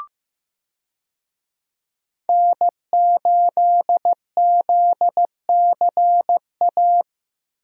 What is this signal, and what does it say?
Morse code, heard as a keyed tone, about 15 wpm, 700 Hz. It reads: N8ZCA
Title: USA Call Signs - 5 Characters - 15wpm